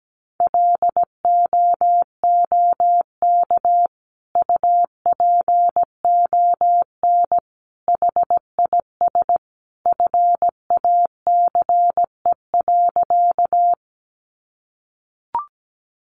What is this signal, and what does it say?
Morse code, heard as a keyed tone, about 17 wpm, 700 Hz. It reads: LOOK UPON HIS FACE.
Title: Sentences from Top 1100 Words 17wpm